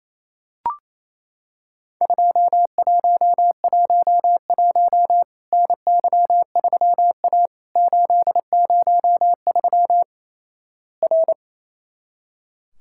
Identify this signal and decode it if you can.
Morse code, heard as a keyed tone, about 28 wpm, 700 Hz. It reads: 2111 NY3A 803 R
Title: Worked All Europe DX Contest - QTC Exchange - 28wpm